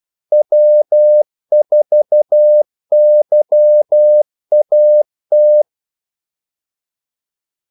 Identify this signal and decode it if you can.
Morse code, heard as a keyed tone, about 12 wpm, 600 Hz. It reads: W4YAT
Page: 2 ekstra ordmellemrum mellem kaldesignalerne